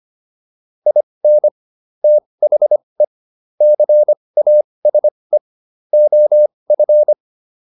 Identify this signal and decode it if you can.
Morse code, heard as a keyed tone, about 25 wpm, spaced as for 20 wpm, 600 Hz. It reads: IN THE CASE OF